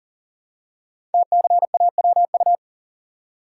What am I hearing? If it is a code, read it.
Morse code, heard as a keyed tone, about 39 wpm, 700 Hz. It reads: TCAWU
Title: Code Groups - 5 Letter:Numbers - 40wpm